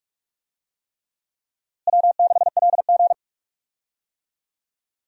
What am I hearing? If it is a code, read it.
Morse code, heard as a keyed tone, about 42 wpm, 700 Hz. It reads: W6LG
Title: USA Call Signs - 4 Characters - Repeated 3x - 45wpm